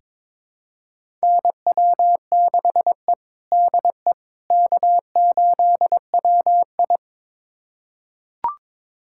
Morse code, heard as a keyed tone, about 22 wpm, 700 Hz. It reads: NW6E DE K8WI
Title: USA Callsign DE Callsign - 4 Characters - 22wpm